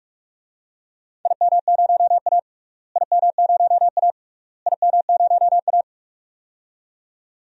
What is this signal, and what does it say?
Morse code, heard as a keyed tone, about 44 wpm, 690 Hz. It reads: IM0A IM0A IM0A
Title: DXCC-100 Call Signs - 4 Characters - Repeated 3x - 45wpm